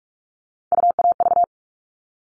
Morse code, heard as a keyed tone, about 44 wpm, 700 Hz. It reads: UAV
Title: All Cut Numbers - 3-Digit 45wpm